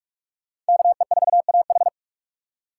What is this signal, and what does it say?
Morse code, heard as a keyed tone, about 41 wpm, 700 Hz. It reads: KE4AH